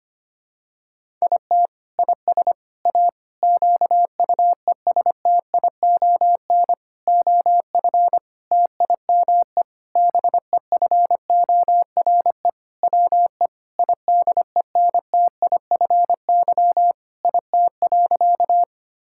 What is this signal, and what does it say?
Morse code, heard as a keyed tone, about 25 wpm, 700 Hz. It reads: IT IS A QUESTION OF TIME BEFORE WE IDENTIFY IT.
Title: Sentences from Top 1000 Words - Review Part 6 - 25wpm